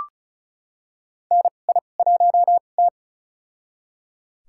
Morse code, heard as a keyed tone, about 36 wpm, spaced as for 25 wpm, 700 Hz. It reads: NI1T
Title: USA Call Signs - 2x Character Spacing - 35wpm